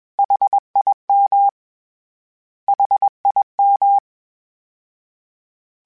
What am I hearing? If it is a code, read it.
Morse code, heard as a keyed tone, about 21 wpm, 800 Hz. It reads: HIM HIM
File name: Short Runs / Short Runs-mp3s / LongContrasts @20